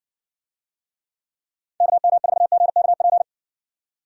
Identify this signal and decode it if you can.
Morse code, heard as a keyed tone, about 60 wpm, 700 Hz. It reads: KM4GKP